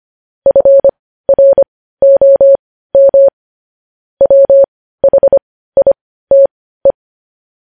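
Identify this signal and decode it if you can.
Morse code, heard as a keyed tone, about 25 wpm, spaced as for 15 wpm, 550 Hz. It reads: FROM WHITE
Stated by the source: Morse news headlines sent at 15 words per minute.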